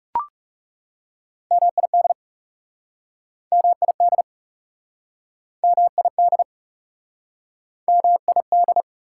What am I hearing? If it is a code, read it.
Morse code, heard as a keyed tone, about 40 wpm, 690 Hz. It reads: MID MID MID MID